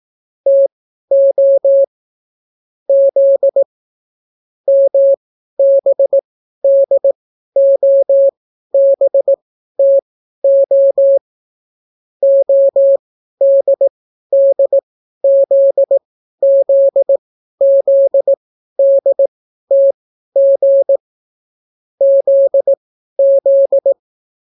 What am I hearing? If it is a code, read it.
Morse code, heard as a keyed tone, about 18 wpm, spaced as for 12 wpm, 550 Hz. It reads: TO Z MBDOBTO ODDZZZDTG ZZ